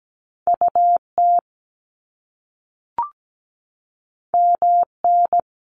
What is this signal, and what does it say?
Morse code, heard as a keyed tone, about 17 wpm, 700 Hz. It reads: UT MN
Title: US States 17wpm